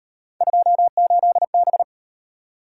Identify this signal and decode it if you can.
Morse code, heard as a keyed tone, about 34 wpm, 700 Hz. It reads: 28B